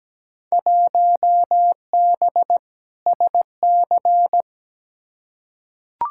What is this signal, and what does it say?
Morse code, heard as a keyed tone, about 17 wpm, 700 Hz. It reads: 1B SC
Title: Field Day Exchanges Encore - 17wpm